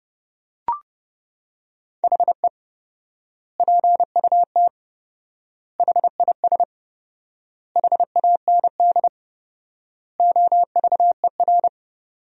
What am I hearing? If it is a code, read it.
Morse code, heard as a keyed tone, about 32 wpm, 700 Hz. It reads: HE PUT HIS HAND OVER
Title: 4x Word-Spacing - Sentences from Top 400 Words - 30wpm